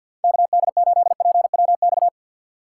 Morse code, heard as a keyed tone, about 53 wpm, 700 Hz. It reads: KD8PWX
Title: USA Call Signs - 6 Characters - 4x Character Spacing - 50wpm